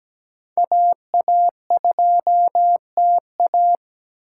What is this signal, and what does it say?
Morse code, heard as a keyed tone, about 17 wpm, 700 Hz. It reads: AA2TA
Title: USA Call Signs - 8x Character Spacing - 17wpm